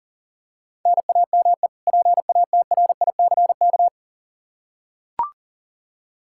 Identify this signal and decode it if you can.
Morse code, heard as a keyed tone, about 40 wpm, 700 Hz. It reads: NAME PATRICK
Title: K1USN SST QSO Practice 40wpm